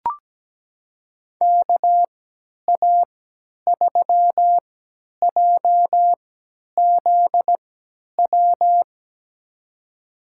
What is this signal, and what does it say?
Morse code, heard as a keyed tone, about 17 wpm, spaced as for 10 wpm, 700 Hz. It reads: KA3JZW
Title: USA Call Signs - 6 Characters - 3x Character Spacing - 17wpm